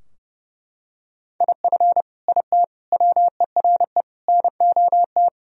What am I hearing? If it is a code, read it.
Morse code, heard as a keyed tone, about 30 wpm, 700 Hz. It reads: IF IT WERE NOT
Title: Sentences from Top 1000 Words - Review Part 2 - 30wpm